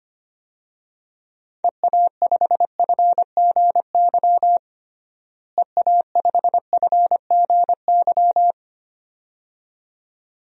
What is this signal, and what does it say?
Morse code, heard as a keyed tone, about 25 wpm, 700 Hz. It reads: EA5FGY EA5FGY
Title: DXCC-100 Call Signs - 6 Characters - Repeated 2x - 25wpm